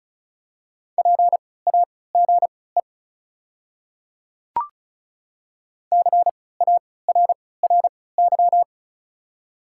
Morse code, heard as a keyed tone, about 36 wpm, spaced as for 20 wpm, 700 Hz. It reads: PAGE CARRY
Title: 3x Character Spacing - Top 300 Words - 35wpm